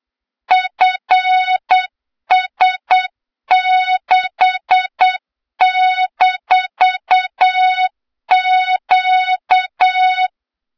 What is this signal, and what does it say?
Morse code, heard as a keyed tone, about 8 wpm, 3700 Hz, with noise. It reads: FS6-Q
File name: Morze-HA2MN / hang / 23 GYAKORLAT